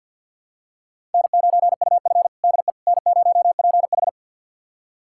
Morse code, heard as a keyed tone, about 52 wpm, 700 Hz. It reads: N9UF DE N0PH
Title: USA Callsign DE Callsign - 4 Characters - 50wpm